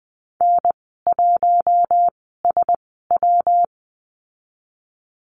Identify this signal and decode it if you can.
Morse code, heard as a keyed tone, about 19 wpm, spaced as for 14 wpm, 700 Hz. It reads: N1SW
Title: 173 Lesson 28g - Number 9 - Call Signs 20wpm